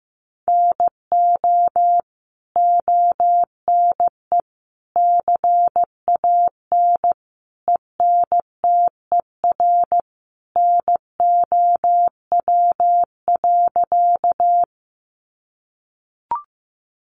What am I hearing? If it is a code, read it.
Morse code, heard as a keyed tone, about 15 wpm, 700 Hz. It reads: NO ONE CAN ENTER NOW.
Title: Sentences from Top 1300 Words 15wpm